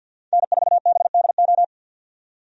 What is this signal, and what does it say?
Morse code, heard as a keyed tone, about 50 wpm, 700 Hz. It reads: N4BDO
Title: USA Call Signs - 5 Characters - 50wpm